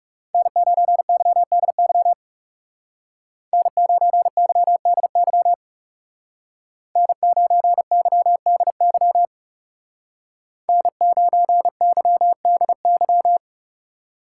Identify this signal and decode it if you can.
Morse code, heard as a keyed tone, about 35 wpm, 700 Hz. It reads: N9YDY N9YDY N9YDY N9YDY
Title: Speed Racing - USA 1x3 Call Signs 30wpm